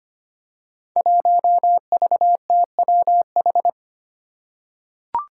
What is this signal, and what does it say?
Morse code, heard as a keyed tone, about 25 wpm, 700 Hz. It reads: 1VTWH
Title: Five Challenging Letter-Number Combinations 25wpm